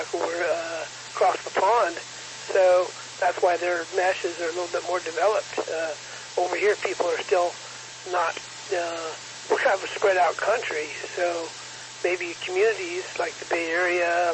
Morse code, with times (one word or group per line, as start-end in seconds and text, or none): none